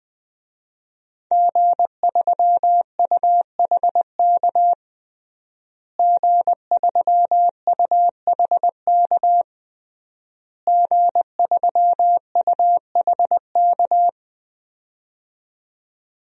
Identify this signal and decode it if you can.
Morse code, heard as a keyed tone, about 21 wpm, 700 Hz. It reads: G3UHK G3UHK G3UHK
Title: DXCC-100 Call Signs - 5 Characters - Repeated 3x - 20wpm